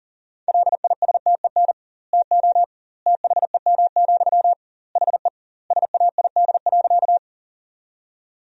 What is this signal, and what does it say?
Morse code, heard as a keyed tone, about 40 wpm, 700 Hz. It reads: LISTEN TO THEM, HE SAID.
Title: 2x Word-Spacing - Sentences from Top 400 Words - 40wpm